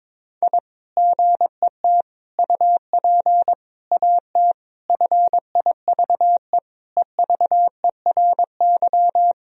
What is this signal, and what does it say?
Morse code, heard as a keyed tone, about 22 wpm, 700 Hz. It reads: I GET UP AT FIVE EVERY
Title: Sentences from Top 1000 Words - Review Part 2 - 22wpm